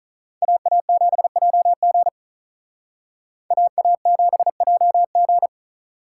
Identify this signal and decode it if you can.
Morse code, heard as a keyed tone, about 42 wpm, 700 Hz. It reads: AA7JG AA7JG
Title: Speed Racing - USA 2x2 Call Signs 35wpm